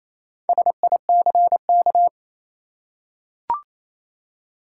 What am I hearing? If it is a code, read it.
Morse code, heard as a keyed tone, about 28 wpm, 700 Hz. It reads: SICK
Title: Four Letter Words - 28wpm